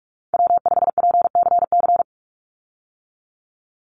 Morse code, heard as a keyed tone, about 44 wpm, 700 Hz. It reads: W5PCC
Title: USA Call Signs 45wpm